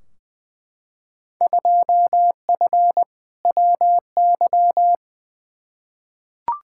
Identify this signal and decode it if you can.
Morse code, heard as a keyed tone, about 20 wpm, 700 Hz. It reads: 2F WY
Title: Field Day Exchanges Encore - 20wpm